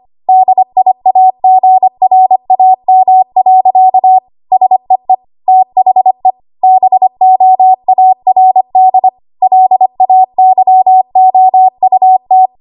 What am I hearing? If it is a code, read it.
Morse code, heard as a keyed tone, about 25 wpm, 750 Hz. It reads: DIAGRAM. SEE THE BOARD LAYOUT
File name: Archive / 25 WPM / 25WPM